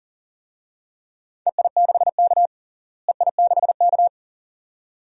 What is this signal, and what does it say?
Morse code, heard as a keyed tone, about 40 wpm, 700 Hz. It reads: EI6K EI6K
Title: DXCC-100 Call Signs - 4 Characters - Repeated 2x - 40wpm